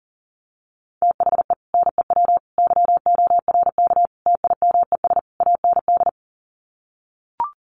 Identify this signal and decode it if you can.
Morse code, heard as a keyed tone, about 37 wpm, 700 Hz. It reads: THE NEW YORK TIMES AND
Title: Sets of 5 Words Encore 40wpm